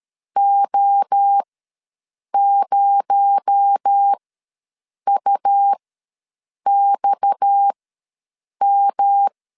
Morse code, heard as a keyed tone, about 13 wpm, spaced as for 7 wpm, 800 Hz. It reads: O0UXM
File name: LEZIONI / 6 PKX0 / 6 PKX0 ws 12wpm 3448